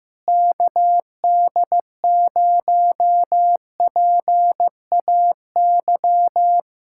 Morse code, heard as a keyed tone, about 15 wpm, 700 Hz. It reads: KD0PAY
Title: Speed-Racing Common QSO Phrases Encore - 15wpm